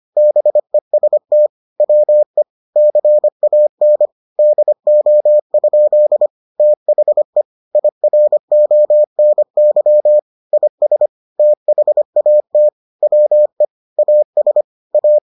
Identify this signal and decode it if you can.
Morse code, heard as a keyed tone, about 25 wpm, 600 Hz. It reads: BEST WE CAN DO? THE IRONY IS THAT WE AS A